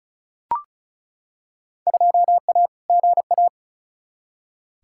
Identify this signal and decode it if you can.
Morse code, heard as a keyed tone, about 34 wpm, 700 Hz. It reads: 2A GA
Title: Field Day Exchanges - 35wpm